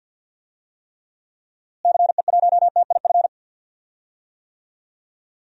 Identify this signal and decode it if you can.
Morse code, heard as a keyed tone, about 51 wpm, 700 Hz. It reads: CE1TIF
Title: DXCC-100 Call Signs - 6 Characters - Repeated 2x - 50wpm